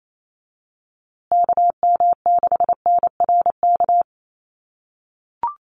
Morse code, heard as a keyed tone, about 28 wpm, 700 Hz. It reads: KM6NRK
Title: USA Call Signs Repeated 2x 28wpm